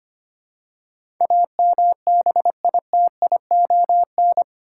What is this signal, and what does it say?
Morse code, heard as a keyed tone, about 25 wpm, 700 Hz. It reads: AMBITION
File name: Eight Letter Words - 25wpm